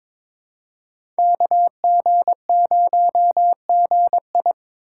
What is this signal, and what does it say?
Morse code, heard as a keyed tone, about 22 wpm, 700 Hz. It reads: KG0GI
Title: USA Call Signs - 5 Characters - 4x Character Spacing - 22wpm